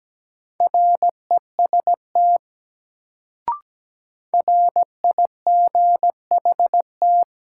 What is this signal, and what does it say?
Morse code, heard as a keyed tone, about 17 wpm, 700 Hz. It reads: REST RIGHT
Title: Sets of 2 Words - New Word Drill - 17wpm